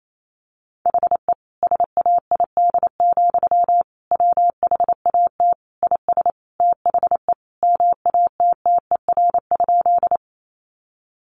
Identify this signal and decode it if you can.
Morse code, heard as a keyed tone, about 28 wpm, 700 Hz. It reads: HE SAID, WHAT IS THE MATTER?
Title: Sentences from Top 700 Words 28wpm